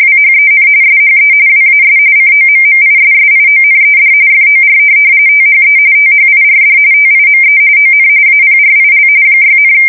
RTTY, about 45.45 baud. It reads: WHEN SHE WAS COMMISSIONED FOR THE THIRD TIME ON APRIL 6,
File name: fsk170 / USS-New-Jersey